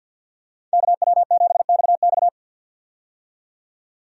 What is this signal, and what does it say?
Morse code, heard as a keyed tone, about 49 wpm, 700 Hz. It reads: KW7XX
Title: USA Call Signs - 5 Characters - 50wpm